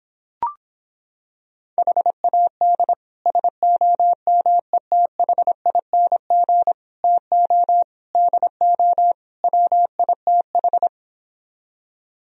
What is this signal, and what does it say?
Morse code, heard as a keyed tone, about 26 wpm, 700 Hz. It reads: HAD SOMETHING TO DO WITH